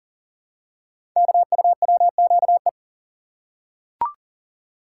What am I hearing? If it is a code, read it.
Morse code, heard as a keyed tone, about 40 wpm, 700 Hz. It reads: KUWQE
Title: Code Groups - 5 Letters - 40wpm